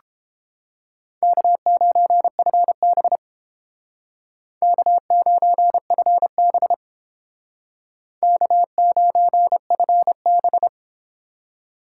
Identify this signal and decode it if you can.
Morse code, heard as a keyed tone, about 30 wpm, 700 Hz. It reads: K9FB K9FB K9FB